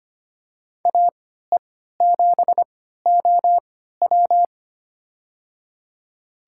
Morse code, heard as a keyed tone, about 25 wpm, spaced as for 14 wpm, 700 Hz. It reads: AE7OW
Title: USA Call Signs - 3x Character Spacing - 25wpm